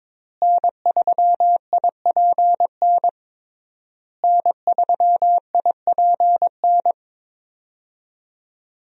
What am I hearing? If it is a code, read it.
Morse code, heard as a keyed tone, about 21 wpm, 700 Hz. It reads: N3IPN N3IPN
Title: USA Call Signs - 5 Characters - Repeated 3x - 22wpm